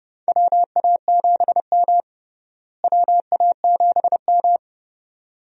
Morse code, heard as a keyed tone, about 30 wpm, 700 Hz. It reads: WA7M WA7M
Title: USA Call Signs - 4 Characters - Repeated 2x - 30wpm